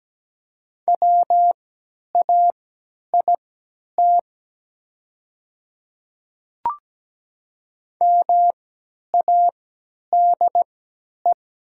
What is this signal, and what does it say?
Morse code, heard as a keyed tone, about 17 wpm, spaced as for 10 wpm, 700 Hz. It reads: WAIT MADE
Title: 3x Character Spacing - Top 500 Words - 17wpm